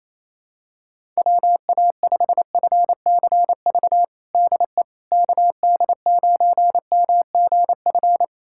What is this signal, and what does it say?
Morse code, heard as a keyed tone, about 28 wpm, 700 Hz. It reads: WA5FCV DE KD9MGF
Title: USA Callsign DE Callsign - 6 Characters - 28wpm